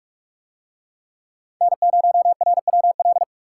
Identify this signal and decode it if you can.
Morse code, heard as a keyed tone, about 40 wpm, 700 Hz. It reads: N0RWL